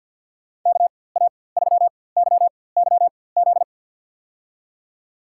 Morse code, heard as a keyed tone, about 53 wpm, spaced as for 24 wpm, 700 Hz. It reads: KA3YYZ
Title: USA Call Signs - 6 Characters - 4x Character Spacing - 50wpm